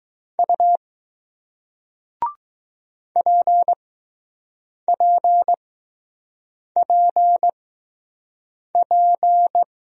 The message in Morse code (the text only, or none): U PPPP